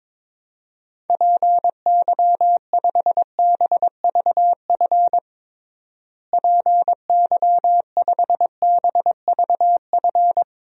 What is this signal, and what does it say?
Morse code, heard as a keyed tone, about 22 wpm, 700 Hz. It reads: PY5BVF PY5BVF
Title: DXCC-100 Call Signs - 6 Characters - Repeated 3x - 22wpm